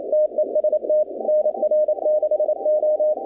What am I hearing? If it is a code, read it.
Morse code, heard as a keyed tone, about 28 wpm, 610 Hz. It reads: TEST NR6O
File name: contests / 2019-cw-wpx-kt5j-k5pi- -2330 0-20